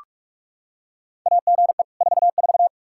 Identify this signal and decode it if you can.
Morse code, heard as a keyed tone, about 43 wpm, 700 Hz. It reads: AGE 44
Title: Practice QSO - Exchange 3 - More Info - 45wpm